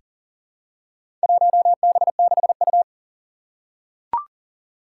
Morse code, heard as a keyed tone, about 38 wpm, 700 Hz. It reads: 1B6U
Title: Four Challenging Letter-Number Combinations 40wpm